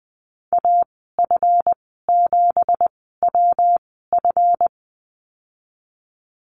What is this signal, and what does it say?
Morse code, heard as a keyed tone, about 20 wpm, spaced as for 15 wpm, 700 Hz. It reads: AF7WF
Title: 173 Lesson 28g - Number 9 - Call Signs 20wpm